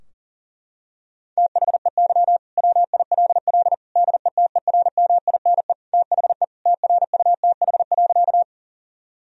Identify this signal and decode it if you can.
Morse code, heard as a keyed tone, about 40 wpm, 700 Hz. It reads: THEY WILL DETERMINE THE TRUTH.
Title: Sentences from Top 1000 Words - Review Part 4 - 40wpm